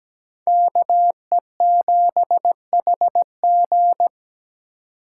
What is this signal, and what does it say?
Morse code, heard as a keyed tone, about 20 wpm, 700 Hz. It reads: KE7HG
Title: Speed Racing - USA 2x2 Call Signs 17wpm